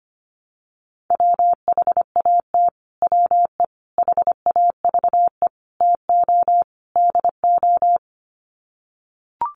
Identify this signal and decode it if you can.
Morse code, heard as a keyed tone, about 25 wpm, 700 Hz. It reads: WHAT WE HAVE TO DO
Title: Sets of 5 Words Encore 25wpm